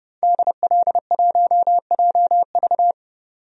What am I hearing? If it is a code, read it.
Morse code, heard as a keyed tone, about 30 wpm, 700 Hz. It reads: DL1JV
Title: DXCC-100 Call Signs - 5 Characters - 30wpm